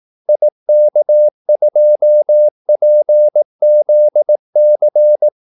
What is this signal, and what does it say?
Morse code, heard as a keyed tone, about 18 wpm, 600 Hz. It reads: IK2PZC
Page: Kaldesignaler 18 wpm